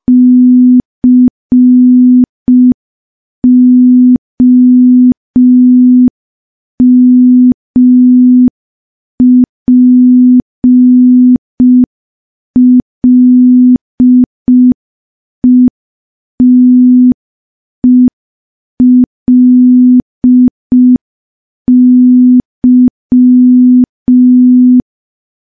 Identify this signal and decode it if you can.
Morse code, heard as a keyed tone, about 5 wpm, 250 Hz. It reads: COMPLETELY
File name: morse / morse-training-05WPM